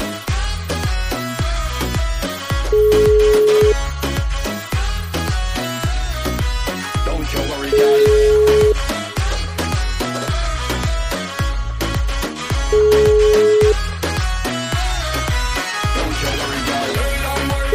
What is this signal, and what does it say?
Ringback tone (Europe, 425 Hz), with noise.